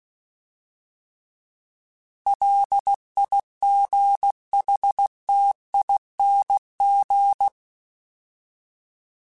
Morse code, heard as a keyed tone, about 16 wpm, 800 Hz. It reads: LIGHTING
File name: Alphabetical / Alphabetical mp3s / Ls @15